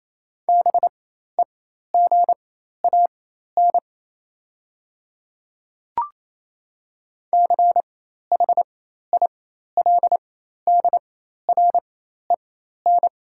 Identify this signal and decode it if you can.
Morse code, heard as a keyed tone, about 28 wpm, spaced as for 13 wpm, 700 Hz. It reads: BEGAN CHILDREN
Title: 4x Character Spacing - Top 300 Words - 28wpm